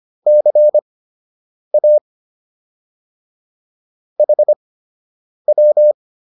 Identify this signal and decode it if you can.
Morse code, heard as a keyed tone, about 25 wpm, spaced as for 8 wpm, 600 Hz. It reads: CA HW